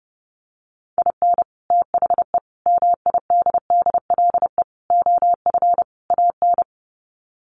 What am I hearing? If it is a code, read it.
Morse code, heard as a keyed tone, about 30 wpm, 700 Hz. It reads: IN THE MIDDLE OF AN
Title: Sets of 5 Words Encore2 30wpm